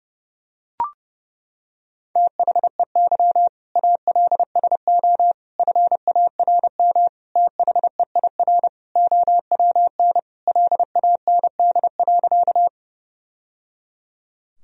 Morse code, heard as a keyed tone, about 31 wpm, 700 Hz. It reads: THEY ALSO FARM THEIR OWN LAND.
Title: Sentences from Top 1000 Words - Review Part 2 - 30wpm